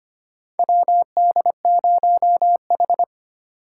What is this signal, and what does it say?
Morse code, heard as a keyed tone, about 25 wpm, 700 Hz. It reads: WD0H
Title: USA Call Signs - 4 Characters - 2x Character Spacing - 25wpm